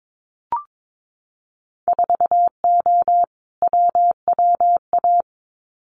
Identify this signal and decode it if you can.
Morse code, heard as a keyed tone, about 22 wpm, 700 Hz. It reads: 4O WWA
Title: Winter Field Day Exchanges 22wpm